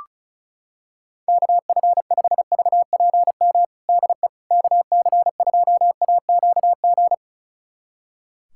Morse code, heard as a keyed tone, about 34 wpm, 700 Hz. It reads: KF5VPM DE KC2AQG
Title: USA Callsign DE Callsign - 6 Characters - 35wpm